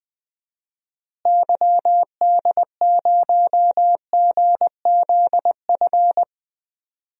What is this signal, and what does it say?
Morse code, heard as a keyed tone, about 20 wpm, 700 Hz. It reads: YD0GZF
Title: DXCC-100 Call Signs - 6 Characters - Repeated 3x - 20wpm